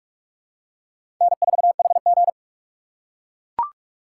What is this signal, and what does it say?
Morse code, heard as a keyed tone, about 43 wpm, 700 Hz. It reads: N4HG
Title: USA Call Signs - 4 Characters - 2x Character Spacing - 45wpm